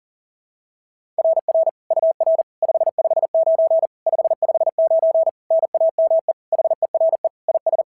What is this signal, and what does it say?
Morse code, heard as a keyed tone, about 40 wpm, 660 Hz. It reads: RR UR 559 559 NAME HERE IS
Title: SOTA QSO Practice 40wpm